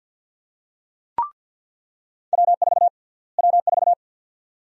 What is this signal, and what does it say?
Morse code, heard as a keyed tone, about 50 wpm, 700 Hz. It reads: W4 W4
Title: USA Call Sign Prefixes - Repeated 2x - 50wpm